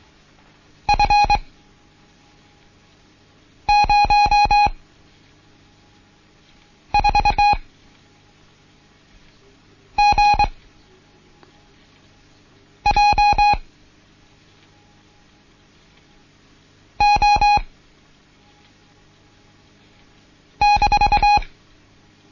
Morse code, heard as a keyed tone, about 23 wpm, 830 Hz. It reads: F 0 4 G J O -
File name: Morsen / Kurs A 16